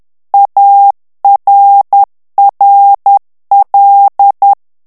Morse code, heard as a keyed tone, about 10 wpm, 800 Hz. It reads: ARRL
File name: ANGLAIS texte clair 3 vitesse 10 mots minute